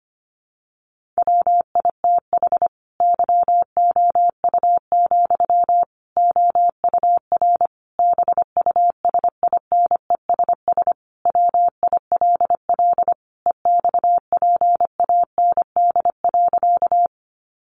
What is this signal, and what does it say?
Morse code, heard as a keyed tone, about 25 wpm, 700 Hz. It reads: WITH YOU, OUR BUSINESS WILL EXPAND.
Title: Sentences from Top 1400 Words 25wpm